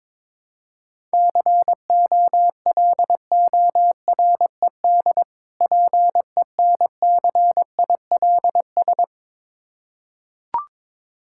Morse code, heard as a keyed tone, about 22 wpm, 700 Hz. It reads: COLORED PENCILS
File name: Difficult Words with Hints - 22wpm